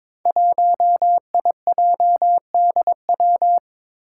Morse code, heard as a keyed tone, about 22 wpm, 700 Hz. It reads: 1IJDW
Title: Five Challenging Letter-Number Combinations 22wpm